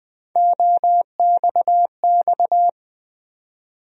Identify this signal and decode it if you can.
Morse code, heard as a keyed tone, about 20 wpm, 700 Hz. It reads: OXX
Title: USA Call Sign Postfixes - 3 Letters 20wpm